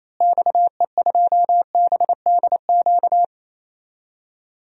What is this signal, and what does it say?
Morse code, heard as a keyed tone, about 28 wpm, 700 Hz. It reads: XE2BDQ
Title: DXCC-100 Call Signs - 6 Characters - Repeated 2x - 28wpm